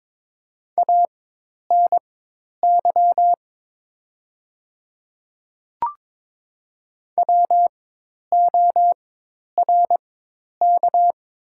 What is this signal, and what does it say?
Morse code, heard as a keyed tone, about 22 wpm, spaced as for 10 wpm, 700 Hz. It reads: ANY WORK
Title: 4x Character Spacing - Top 300 Words - 22wpm